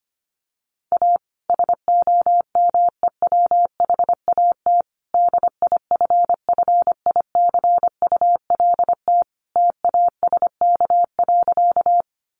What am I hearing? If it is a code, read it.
Morse code, heard as a keyed tone, about 25 wpm, 700 Hz. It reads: A SOMEWHAT DIFFICULT TASK.
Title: Sentences from Top 1300 Words 25wpm